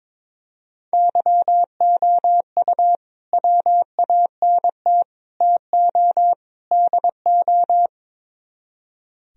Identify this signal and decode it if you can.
Morse code, heard as a keyed tone, about 22 wpm, 700 Hz. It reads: YOU WANT TO DO
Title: Speed-Racing - Sets of 4 Words 22wpm